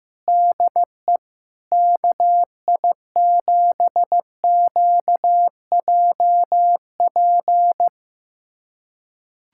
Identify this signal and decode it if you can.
Morse code, heard as a keyed tone, about 15 wpm, 700 Hz. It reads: DE KI7QJP
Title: USA Callsign DE Callsign - 6 Characters - 15wpm